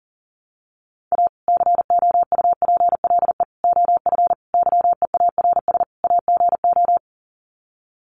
Morse code, heard as a keyed tone, about 40 wpm, 700 Hz. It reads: A COUPLE OF YEARS AGO
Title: Sets of 5 Words 40wpm